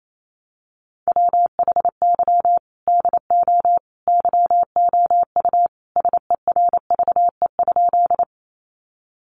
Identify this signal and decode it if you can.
Morse code, heard as a keyed tone, about 28 wpm, 700 Hz. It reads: WHY DO YOU SERVE?
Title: Sentences from Top 400 Words 28wpm